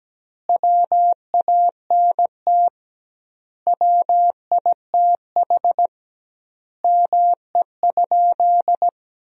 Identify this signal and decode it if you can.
Morse code, heard as a keyed tone, about 17 wpm, 700 Hz. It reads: WANT WITH ME?
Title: 2x Word-Spacing - Sentences from Top 200 Words - 17wpm